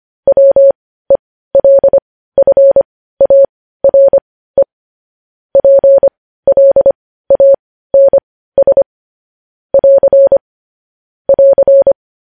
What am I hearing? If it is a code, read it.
Morse code, heard as a keyed tone, about 25 wpm, spaced as for 15 wpm, 550 Hz. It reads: WELFARE PLANS <AR> <AR>
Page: news-headlines-in-morse-code-at-15-wpm